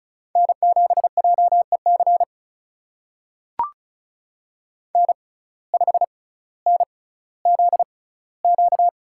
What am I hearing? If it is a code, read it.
Morse code, heard as a keyed tone, about 36 wpm, spaced as for 18 wpm, 700 Hz. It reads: N7JEC N5NZQ
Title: USA Call Signs - 5 Characters - 6x Character Spacing - 35wpm